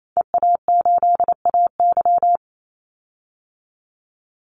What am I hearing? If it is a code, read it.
Morse code, heard as a keyed tone, about 28 wpm, 700 Hz. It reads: EA8AY
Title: Call Signs Repeated 3x - 28wpm